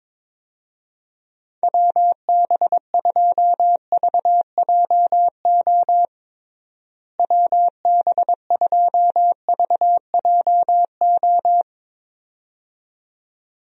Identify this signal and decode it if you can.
Morse code, heard as a keyed tone, about 22 wpm, 700 Hz. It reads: WB2VJO WB2VJO
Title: USA Call Signs - 6 Characters - Repeated 2x - 22wpm